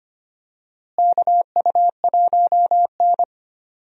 Morse code, heard as a keyed tone, about 25 wpm, 700 Hz. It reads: KU1N
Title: USA Call Signs - 4 Characters - Repeated 2x - 25wpm